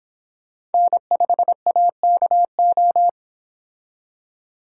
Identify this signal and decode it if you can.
Morse code, heard as a keyed tone, about 28 wpm, 700 Hz. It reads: N5AKO